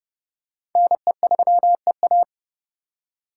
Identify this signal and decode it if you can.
Morse code, heard as a keyed tone, about 28 wpm, 700 Hz. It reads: NE3EA